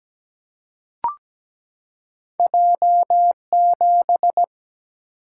Morse code, heard as a keyed tone, about 17 wpm, 700 Hz. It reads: J7
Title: Single-Letter Single-Number 17wpm